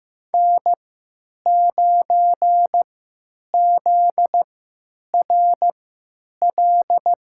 Morse code, heard as a keyed tone, about 15 wpm, spaced as for 8 wpm, 700 Hz. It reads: N9ZRL
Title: USA Call Signs - 5 Characters - 3x Character Spacing - 15wpm